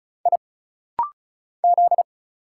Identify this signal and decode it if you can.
Morse code, heard as a keyed tone, about 35 wpm, 700 Hz. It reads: I Z
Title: Single Letters - Rapid-Fire 35wpm